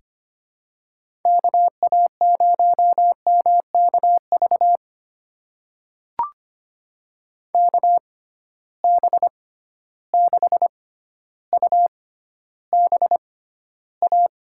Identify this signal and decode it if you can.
Morse code, heard as a keyed tone, about 25 wpm, spaced as for 13 wpm, 700 Hz. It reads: KA0MKV KB6UBA
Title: USA Call Signs - 6 Characters - 6x Character Spacing - 25wpm